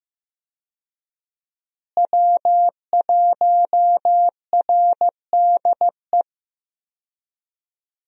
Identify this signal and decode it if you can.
Morse code, heard as a keyed tone, about 15 wpm, 700 Hz. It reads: W1RDE
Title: USA Call Signs - 5 Characters - Repeated 2x - 15wpm